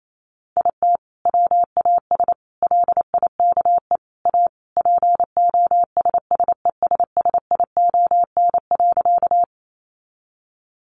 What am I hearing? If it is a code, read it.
Morse code, heard as a keyed tone, about 28 wpm, 700 Hz. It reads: IT WAS LIKE A POSSESSION.
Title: Sentences from Top 1400 Words 28wpm